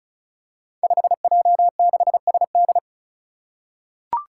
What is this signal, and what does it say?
Morse code, heard as a keyed tone, about 36 wpm, 700 Hz. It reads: 5J6SD
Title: Five Challenging Letter-Number Combinations 35wpm